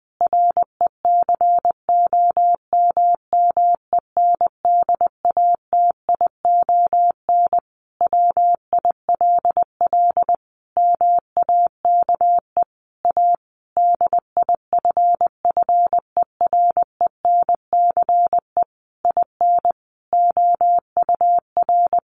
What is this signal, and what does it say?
Morse code, heard as a keyed tone, about 20 wpm, 700 Hz. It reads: RECOMMENDATION WILL MAKE A DIFFERENCE IN OUR
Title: Sentences from Top 1400 Words 20wpm